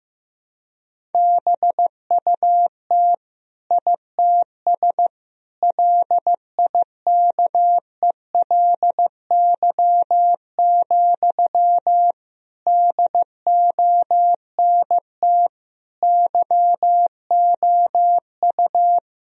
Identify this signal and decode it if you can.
Morse code, heard as a keyed tone, about 16 wpm, 700 Hz. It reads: BUT ITS LIKELY, DONT YOU
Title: Sentences from Top 1000 Words - Review Part 4 - 15wpm